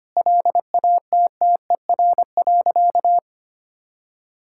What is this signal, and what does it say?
Morse code, heard as a keyed tone, about 25 wpm, 700 Hz. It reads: LATTER.
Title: Sentences from Top 1600 Words 25wpm